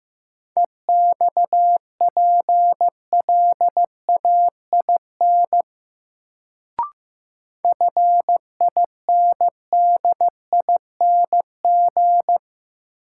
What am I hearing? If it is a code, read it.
Morse code, heard as a keyed tone, about 15 wpm, 700 Hz. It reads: EXPLAIN FINDING
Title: Sentences from Top 1100 Words - New Word Drill Alt - 15wpm